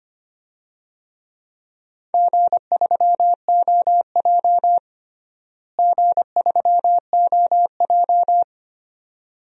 Morse code, heard as a keyed tone, about 25 wpm, 700 Hz. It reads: G3OJ G3OJ
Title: DXCC-100 Call Signs - 4 Characters - Repeated 2x - 25wpm